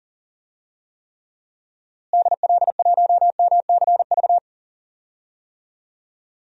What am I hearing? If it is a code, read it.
Morse code, heard as a keyed tone, about 40 wpm, 700 Hz. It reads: DL1MCV
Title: DXCC-100 Call Signs - 6 Characters - Repeated 3x - 40wpm